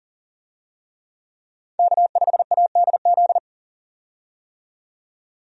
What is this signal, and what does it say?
Morse code, heard as a keyed tone, about 40 wpm, 700 Hz. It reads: K5ADZ
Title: USA Call Signs - 5 Characters - 40wpm